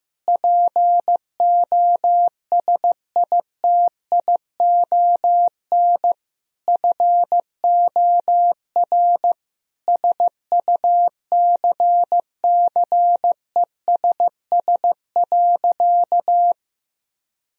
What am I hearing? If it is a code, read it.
Morse code, heard as a keyed tone, about 16 wpm, 700 Hz. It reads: POSITION FOR SUCCESS.
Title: Sentences from Top 1000 Words - Review Part 4 - 15wpm